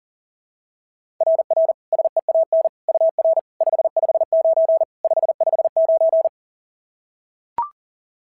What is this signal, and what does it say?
Morse code, heard as a keyed tone, about 40 wpm, 690 Hz. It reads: RR SEAN UR 559 559
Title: SOTA QSO Practice 40wpm